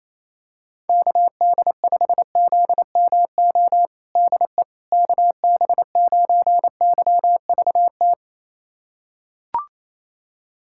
Morse code, heard as a keyed tone, about 27 wpm, 700 Hz. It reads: KD5ZMO DE KB9YVT
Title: USA Callsign DE Callsign - 6 Characters - 28wpm